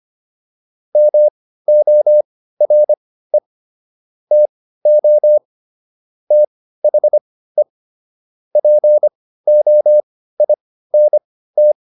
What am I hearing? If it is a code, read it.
Morse code, heard as a keyed tone, about 25 wpm, spaced as for 15 wpm, 600 Hz. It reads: MORE TO THE POINT